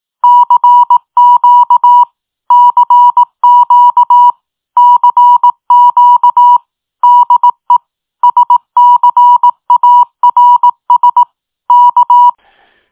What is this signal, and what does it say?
Morse code, heard as a keyed tone, about 18 wpm, 970 Hz. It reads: CQ CQ CQ DE SCARS K
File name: tin-upgrade / QSO 2023 06 15 .mp3